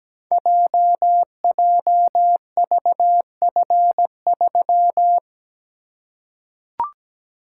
Morse code, heard as a keyed tone, about 17 wpm, 690 Hz. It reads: JJVF3
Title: Code Groups - 5 Letter:Numbers - 17wpm